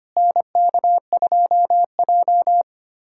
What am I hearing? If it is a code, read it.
Morse code, heard as a keyed tone, about 25 wpm, 700 Hz. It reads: NK2J
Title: USA Callsign DE Callsign - 4 Characters - 25wpm